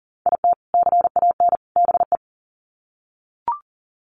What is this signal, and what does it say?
Morse code, heard as a keyed tone, about 40 wpm, 700 Hz. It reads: IT CAN BE
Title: Sets of 3 Words 40wpm